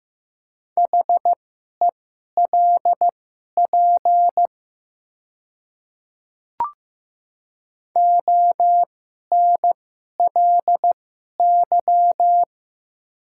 Morse code, heard as a keyed tone, about 15 wpm, spaced as for 11 wpm, 700 Hz. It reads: HELP ONLY
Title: 2x Character Spacing - Top 200 Words - 15wpm